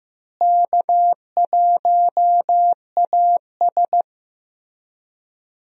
Morse code, heard as a keyed tone, about 15 wpm, 700 Hz. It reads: K1AS
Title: USA Call Signs - 4 Characters - Repeated 3x - 15wpm